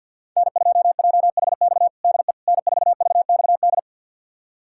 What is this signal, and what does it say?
Morse code, heard as a keyed tone, about 47 wpm, 700 Hz. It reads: N2JHX DE N4VXD
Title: USA Callsign DE Callsign - 5 Characters - 50wpm